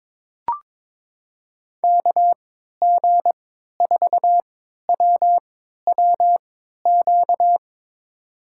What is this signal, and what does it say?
Morse code, heard as a keyed tone, about 22 wpm, spaced as for 13 wpm, 700 Hz. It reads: KG4WWQ
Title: USA Call Signs - 3x Character Spacing - 22wpm